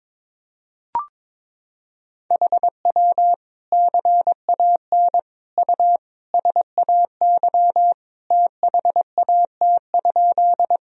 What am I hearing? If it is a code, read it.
Morse code, heard as a keyed tone, about 22 wpm, 700 Hz. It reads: HW CAN U SAY THAT?
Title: Sentences from Top 200 Words and Abbreviations 22wpm